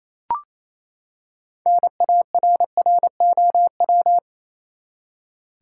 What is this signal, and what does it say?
Morse code, heard as a keyed tone, about 28 wpm, 700 Hz. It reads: NARROW
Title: Six Letter Words - 28wpm